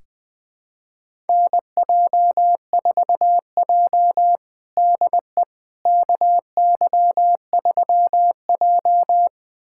Morse code, heard as a keyed tone, about 21 wpm, 700 Hz. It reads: NJ4J DE KY3J
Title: USA Callsign DE Callsign - 4 Characters - 20wpm